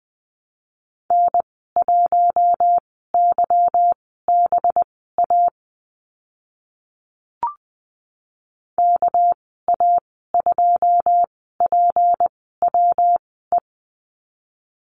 Morse code, heard as a keyed tone, about 20 wpm, spaced as for 14 wpm, 700 Hz. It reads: N1YBA KA2PWE
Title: 207 Lesson 33g - Letter j - Call Signs 20wpm